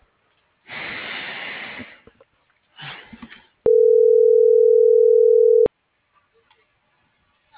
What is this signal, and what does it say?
Ringback tone (North America), 440 + 480 Hz.